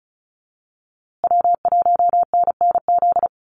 Morse code, heard as a keyed tone, about 34 wpm, 700 Hz. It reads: W1NNZ
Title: USA Call Signs Repeated 3x 35wpm